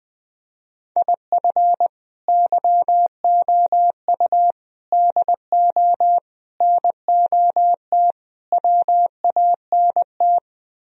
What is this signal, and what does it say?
Morse code, heard as a keyed tone, about 20 wpm, 700 Hz. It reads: IF YOU DO NOT WANT
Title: Speed-Racing - Sets of 5 Words 17wpm